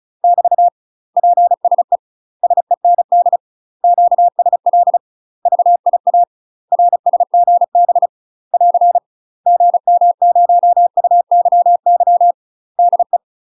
Morse code, heard as a keyed tone, about 34 wpm, 700 Hz. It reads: <BT> PSE SEND QSL VIA RSGB <AR> GM0UYY DE
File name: UK MASTER QSOs TRAINING/35WPM